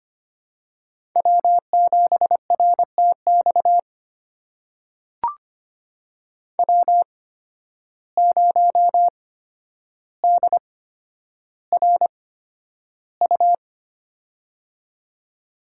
Morse code, heard as a keyed tone, about 25 wpm, spaced as for 11 wpm, 700 Hz. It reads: W7RTX W0DRU
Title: USA Call Signs - 5 Characters - 8x Character Spacing - 25wpm